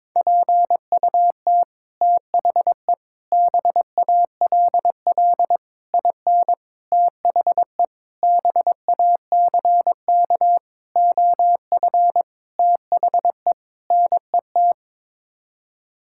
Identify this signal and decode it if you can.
Morse code, heard as a keyed tone, about 22 wpm, 700 Hz. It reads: PUT THE BALL IN THE BACK OF THE NET
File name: Sets of 9 Words - 22wpm